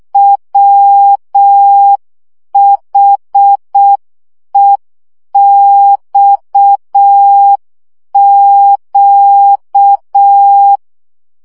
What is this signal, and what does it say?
Morse code, heard as a keyed tone, about 6 wpm, 800 Hz. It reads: WHEXQ
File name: morze vizsga 2011-2 5